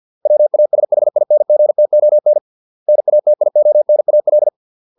Morse code, heard as a keyed tone, about 50 wpm, 600 Hz. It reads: WASHINGTON NATIONAL